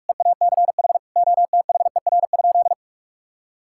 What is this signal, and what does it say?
Morse code, heard as a keyed tone, about 46 wpm, 700 Hz. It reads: EACH OTHER?
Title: Sentences from Top 1000 Words - Review Part 6 - 45wpm